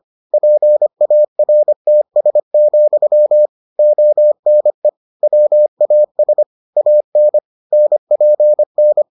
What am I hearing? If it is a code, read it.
Morse code, heard as a keyed tone, about 25 wpm, 600 Hz. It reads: PARTS, ONE WAS AN NPN